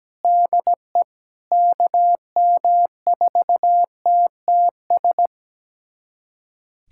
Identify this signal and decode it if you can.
Morse code, heard as a keyed tone, about 17 wpm, 700 Hz. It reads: DE KM4TTS
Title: USA Callsign DE Callsign - 6 Characters - 17wpm